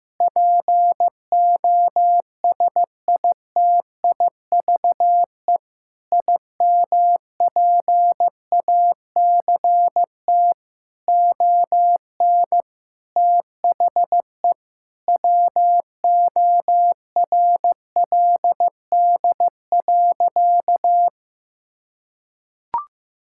Morse code, heard as a keyed tone, about 15 wpm, 700 Hz. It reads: POSITIVE IMPACT ON THE WORLD.
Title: Sentences from Top 1000 Words - Review Part 6 - 15wpm